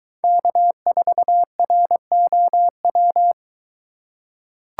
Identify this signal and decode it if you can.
Morse code, heard as a keyed tone, about 23 wpm, 700 Hz. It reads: K4ROW